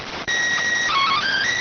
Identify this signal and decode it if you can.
SSTV (slow-scan television), PD 290.